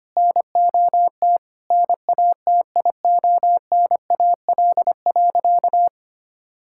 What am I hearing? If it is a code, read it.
Morse code, heard as a keyed tone, about 25 wpm, 700 Hz. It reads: NOT NATIONAL.
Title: Sentences from Top 1000 Words - Review Part 5 - 25wpm